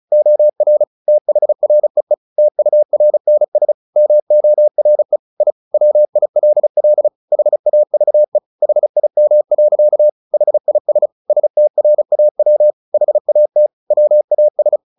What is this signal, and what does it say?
Morse code, heard as a keyed tone, about 35 wpm, 600 Hz. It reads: OR THREE TURNS MORE I WILL HAVE HIM. HIS STRAW HAT WAS
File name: the old man and the sea 2 35wpm